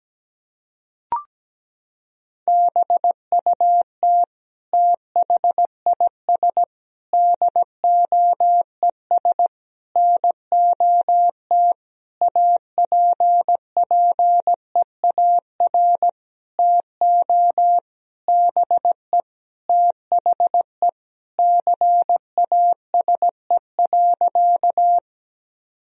Morse code, heard as a keyed tone, about 17 wpm, 700 Hz. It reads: BUT THIS DOES NOT APPEAR TO BE THE CASE.
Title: Sentences from Top 1000 Words - Review Part 2 - 17wpm